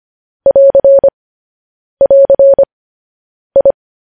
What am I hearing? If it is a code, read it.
Morse code, heard as a keyed tone, about 25 wpm, 550 Hz. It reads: <AR> <AR> I
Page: news-headlines-in-morse-code-at-15-wpm